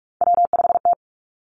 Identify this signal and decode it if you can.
Morse code, heard as a keyed tone, about 46 wpm, 700 Hz. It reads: W5T
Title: Three Challenging Letter-Number Combinations 45wpm